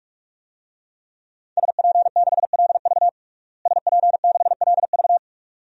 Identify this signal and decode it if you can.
Morse code, heard as a keyed tone, about 46 wpm, 700 Hz. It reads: SP6LV SP6LV
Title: DXCC-100 Call Signs - 5 Characters - Repeated 2x - 45wpm